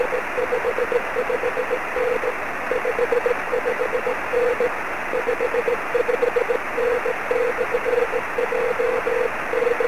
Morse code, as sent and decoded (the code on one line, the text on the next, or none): none